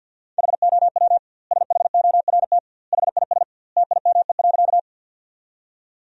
Morse code, heard as a keyed tone, about 50 wpm, 700 Hz. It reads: HOW SHORT HIS TIME.
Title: 2x Word-Spacing - Sentences from Top 400 Words - 50wpm